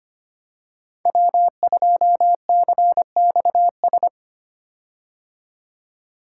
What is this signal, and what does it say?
Morse code, heard as a keyed tone, about 25 wpm, 700 Hz. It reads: W2CXS
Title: Code Groups - 5 Letter:Numbers - 25wpm